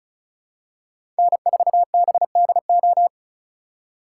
Morse code, heard as a keyed tone, about 34 wpm, 700 Hz. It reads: N4BDO
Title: USA Call Signs - 5 Characters - 4x Character Spacing - 35wpm